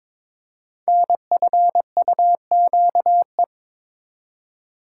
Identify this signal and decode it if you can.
Morse code, heard as a keyed tone, about 23 wpm, 700 Hz. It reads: NFUQE
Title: Code Groups - 5 Letter:Numbers - 22wpm